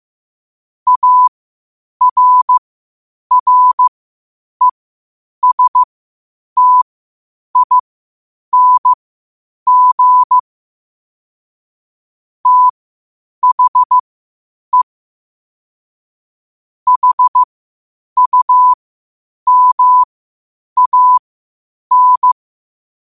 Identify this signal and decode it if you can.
Morse code, heard as a keyed tone, about 15 wpm, spaced as for 8 wpm, 1000 Hz. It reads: ARRESTING THE HUMAN
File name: qotdfiles / 8wpm